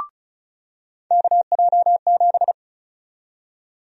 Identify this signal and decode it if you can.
Morse code, heard as a keyed tone, about 36 wpm, 700 Hz. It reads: KJ7
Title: USA Call Sign Prefixes - 35wpm